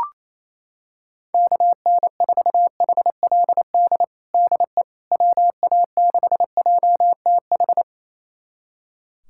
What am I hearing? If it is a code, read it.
Morse code, heard as a keyed tone, about 28 wpm, 700 Hz. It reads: KN4HLD DE WA6JTH
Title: USA Callsign DE Callsign 28wpm